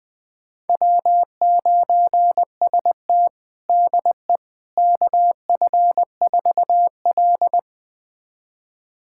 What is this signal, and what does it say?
Morse code, heard as a keyed tone, about 20 wpm, 700 Hz. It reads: W9ST DE KF4L
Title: USA Callsign DE Callsign - 4 Characters - 20wpm